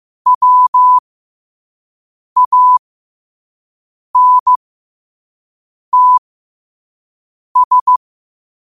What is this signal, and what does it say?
Morse code, heard as a keyed tone, about 15 wpm, spaced as for 5 wpm, 1000 Hz. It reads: WANTS
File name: qotdfiles / 5wpm